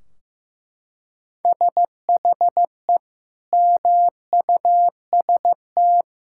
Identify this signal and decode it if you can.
Morse code, heard as a keyed tone, about 15 wpm, 700 Hz. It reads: SHE MUST
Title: Sentences from Top 1600 Words 15wpm